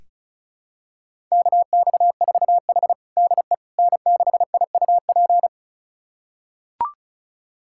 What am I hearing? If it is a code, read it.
Morse code, heard as a keyed tone, about 36 wpm, 700 Hz. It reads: KX4H DE N6IUP
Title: USA Callsign DE Callsign 35wpm